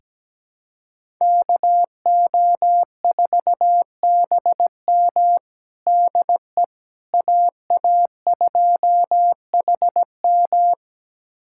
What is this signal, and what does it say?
Morse code, heard as a keyed tone, about 17 wpm, 700 Hz. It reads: KO4BM DE AA2HM
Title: USA Callsign DE Callsign - 5 Characters - 17wpm